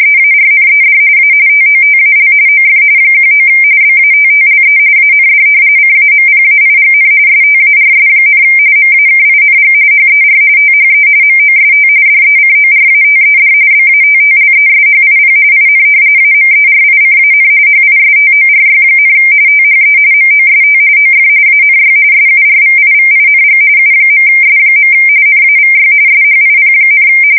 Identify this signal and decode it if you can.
RTTY, about 45.45 baud. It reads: EARNED A TOTAL OF 19 BATTLE AND CAMPAIGN STARS, MAKING HER THE MOST DECORATED BATTLESHIP IN AMERICAN HISTORY, THE MOST OF ANY SURVIVING U.S. NAVY SHIP,
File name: fsk170 / battleshipnj